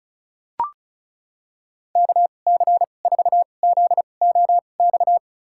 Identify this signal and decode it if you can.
Morse code, heard as a keyed tone, about 36 wpm, spaced as for 25 wpm, 700 Hz. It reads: KC4ZOX
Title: USA Call Signs - 6 Characters - 2x Character Spacing - 35wpm